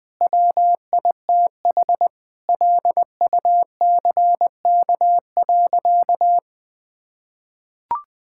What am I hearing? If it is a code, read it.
Morse code, heard as a keyed tone, about 21 wpm, 700 Hz. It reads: WITH LUCK.
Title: Sentences with 5-Word Phrases - 20wpm